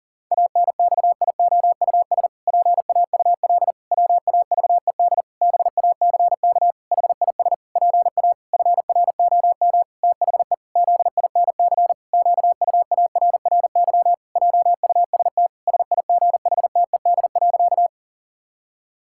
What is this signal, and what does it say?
Morse code, heard as a keyed tone, about 40 wpm, 700 Hz. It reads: ANXIOUS PAUL WAVED BACK HIS PA FROM THE ZINC QUARRY JUST SIGHTED.
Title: Pangram Challenge v2 - 40wpm